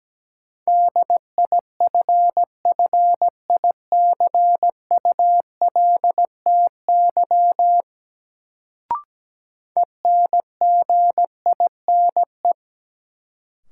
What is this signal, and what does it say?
Morse code, heard as a keyed tone, about 17 wpm, 700 Hz. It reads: DIFFICULTY ENGINE
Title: Sentences from Top 1500 Words - New Word Drill Alt - 17wpm